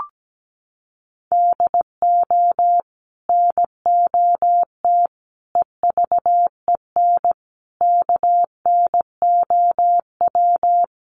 Morse code, heard as a keyed tone, about 17 wpm, 700 Hz. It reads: DO NOT EVEN KNOW
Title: Sets of 5 Words Encore 17wpm